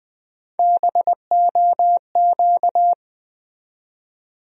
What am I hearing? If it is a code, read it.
Morse code, heard as a keyed tone, about 20 wpm, 700 Hz. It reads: BOQ
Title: USA Call Sign Postfixes - 3 Letters 20wpm